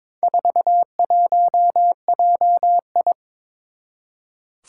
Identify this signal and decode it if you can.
Morse code, heard as a keyed tone, about 22 wpm, 700 Hz. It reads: 41JI
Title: Four Challenging Letter-Number Combinations 22wpm